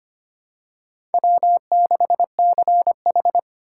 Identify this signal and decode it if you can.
Morse code, heard as a keyed tone, about 25 wpm, 700 Hz. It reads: W6CH
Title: USA Call Signs - 3x Character Spacing - 25wpm